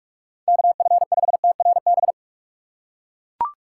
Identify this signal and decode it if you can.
Morse code, heard as a keyed tone, about 41 wpm, 700 Hz. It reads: KF5TRB
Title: USA Call Signs - 6 Characters - 2x Character Spacing - 45wpm